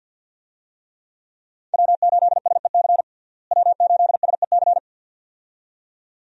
Morse code, heard as a keyed tone, about 52 wpm, 700 Hz. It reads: W8SEC W8SEC
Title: USA Call Signs - 5 Characters - Repeated 2x - 50wpm